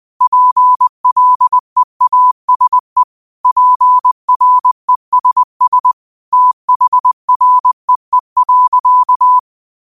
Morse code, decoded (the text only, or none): PLEASE PRESS THREE.